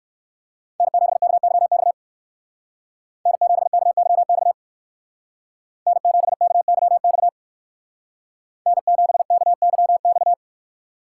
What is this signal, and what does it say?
Morse code, heard as a keyed tone, about 58 wpm, 700 Hz. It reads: N7KYX N7KYX N7KYX N7KYX